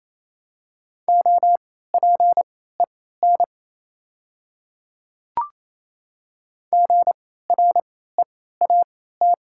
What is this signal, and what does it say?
Morse code, heard as a keyed tone, about 28 wpm, spaced as for 16 wpm, 700 Hz. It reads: OPEN GREAT
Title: 3x Character Spacing - Top 300 Words - 28wpm